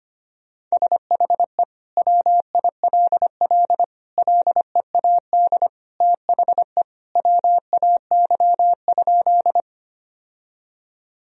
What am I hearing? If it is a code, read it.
Morse code, heard as a keyed tone, about 25 wpm, 700 Hz. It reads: SHE WILL LEAD THE WAY?
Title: Sentences from Top 1000 Words - Review Part 2 - 25wpm